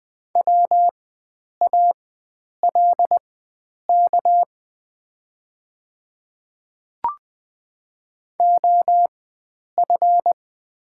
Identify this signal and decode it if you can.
Morse code, heard as a keyed tone, about 20 wpm, spaced as for 9 wpm, 700 Hz. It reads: WALK OF
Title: 4x Character Spacing - Top 500 Words - 20wpm